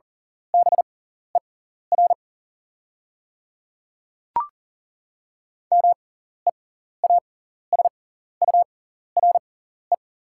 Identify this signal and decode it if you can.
Morse code, heard as a keyed tone, about 40 wpm, spaced as for 14 wpm, 700 Hz. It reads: BER MEASURE
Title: 6x Character Spacing - Top 500 Words - 40wpm